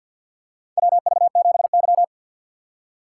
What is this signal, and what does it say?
Morse code, heard as a keyed tone, about 48 wpm, 700 Hz. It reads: WV7Y
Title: USA Call Signs - 4 Characters - 2x Character Spacing - 50wpm